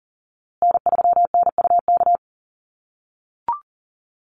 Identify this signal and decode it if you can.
Morse code, heard as a keyed tone, about 40 wpm, 700 Hz. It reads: N3NUK
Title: USA Call Signs Repeated 2x 40wpm